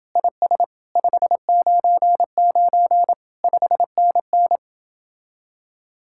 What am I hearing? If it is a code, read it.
Morse code, heard as a keyed tone, about 27 wpm, 700 Hz. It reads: IS 599 5NN